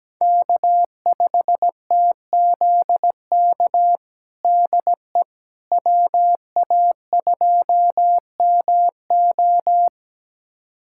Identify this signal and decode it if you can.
Morse code, heard as a keyed tone, about 17 wpm, 700 Hz. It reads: K5TZK DE WA2MO
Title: USA Callsign DE Callsign - 5 Characters - 17wpm